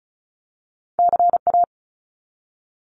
Morse code, heard as a keyed tone, about 34 wpm, 700 Hz. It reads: CA
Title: US States Repeated 3x 35wpm